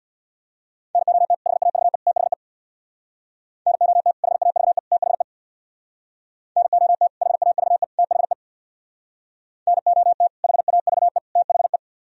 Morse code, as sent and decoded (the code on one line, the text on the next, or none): -. --- - / .... .- ...- . / - .... / -. --- - / .... .- ...- . / - .... / -. --- - / .... .- ...- . / - .... / -. --- - / .... .- ...- . / - .... .
NOT HAVE THE NOT HAVE THE NOT HAVE THE NOT HAVE THE